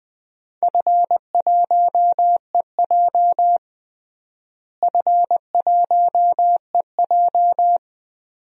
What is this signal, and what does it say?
Morse code, heard as a keyed tone, about 20 wpm, 700 Hz. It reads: F1EJ F1EJ
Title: DXCC-100 Call Signs - 4 Characters - Repeated 2x - 20wpm